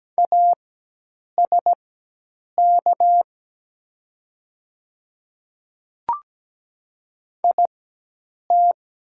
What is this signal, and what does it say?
Morse code, heard as a keyed tone, about 17 wpm, spaced as for 8 wpm, 700 Hz. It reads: ASK IT
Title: 4x Character Spacing - Top 200 Words - 17wpm